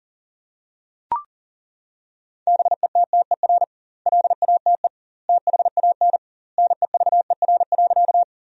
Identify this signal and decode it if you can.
Morse code, heard as a keyed tone, about 40 wpm, 700 Hz. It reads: BETTER LATE THAN NEVER.
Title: 2x Word-Spacing - Sentences from Top 400 Words - 40wpm